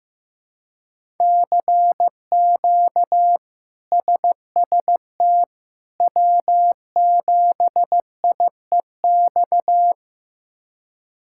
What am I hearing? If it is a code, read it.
Morse code, heard as a keyed tone, about 16 wpm, 700 Hz. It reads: CQ SST W7IEX
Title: K1USN SST QSO Practice - 15wpm